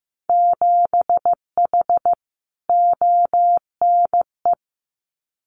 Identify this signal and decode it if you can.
Morse code, heard as a keyed tone, about 15 wpm, 700 Hz. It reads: 7H ONE
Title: Winter Field Day Exchanges Encore 15wpm